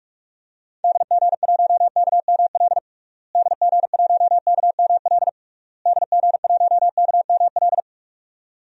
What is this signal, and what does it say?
Morse code, heard as a keyed tone, about 46 wpm, 700 Hz. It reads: DG1KML DG1KML DG1KML
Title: DXCC-100 Call Signs - 6 Characters - Repeated 3x - 45wpm